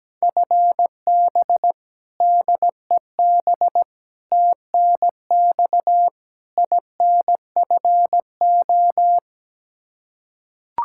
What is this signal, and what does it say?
Morse code, heard as a keyed tone, about 17 wpm, 700 Hz. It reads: FB DEB TNX INFO
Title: Practice QSO - Exchange 3 - More Info - 17wpm